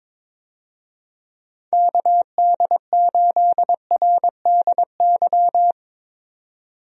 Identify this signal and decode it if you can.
Morse code, heard as a keyed tone, about 22 wpm, 700 Hz. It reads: KD8RDY
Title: USA Call Signs - 6 Characters - 22wpm